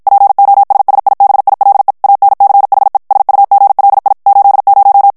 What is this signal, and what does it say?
Morse code, heard as a keyed tone, about 52 wpm, 800 Hz. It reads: POSSIBILE ANCHE SUGLI 80
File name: morse mp3 / ITALIEN texte clair 1 vitesse 50 mots minute